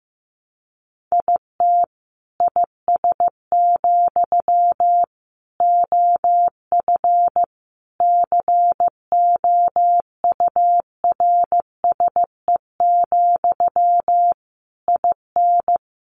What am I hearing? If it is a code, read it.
Morse code, heard as a keyed tone, about 15 wpm, 700 Hz. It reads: IT IS, OF COURSE, IN
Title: Sentences from Top 1200 Words 15wpm